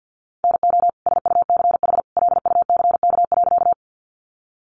Morse code, heard as a keyed tone, about 50 wpm, 700 Hz. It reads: NO SUCH LUCK.
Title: Sentences from Top 1500 Words 50wpm